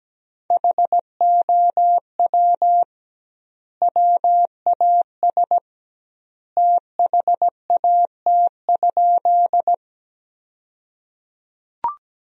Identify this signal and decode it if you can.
Morse code, heard as a keyed tone, about 17 wpm, 700 Hz. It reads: HOW WAS THAT?
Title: 2x Word-Spacing - Sentences from Top 100 Words - 17wpm